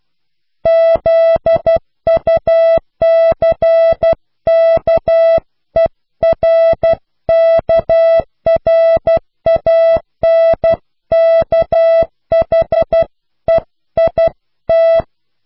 Morse code, heard as a keyed tone, about 12 wpm, 660 Hz. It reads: ZUCKERKRANKHEIT
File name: cw / Pseudo woorden (15)